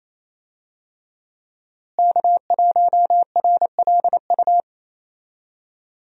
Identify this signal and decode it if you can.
Morse code, heard as a keyed tone, about 28 wpm, 700 Hz. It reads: K1RLU
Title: USA Call Signs - 5 Characters - Repeated 2x - 28wpm